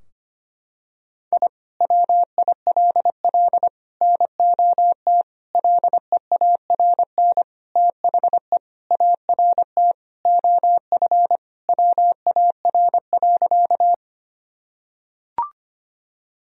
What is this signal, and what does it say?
Morse code, heard as a keyed tone, about 25 wpm, 700 Hz. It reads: I WILL NOT LEARN THE ART OF WAR.
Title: Sentences from Top 1000 Words - Review Part 3 - 25wpm